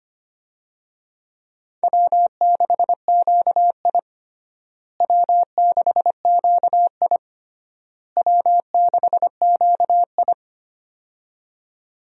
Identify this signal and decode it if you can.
Morse code, heard as a keyed tone, about 25 wpm, 700 Hz. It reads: W6QI W6QI W6QI
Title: USA Call Signs - 4 Characters - Repeated 3x - 25wpm